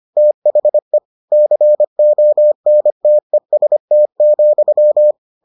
Morse code, heard as a keyed tone, about 25 wpm, 600 Hz. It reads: THE CONTEST,